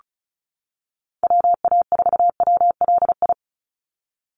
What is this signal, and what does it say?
Morse code, heard as a keyed tone, about 34 wpm, 700 Hz. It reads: WA4WLI
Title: Call Signs - 35wpm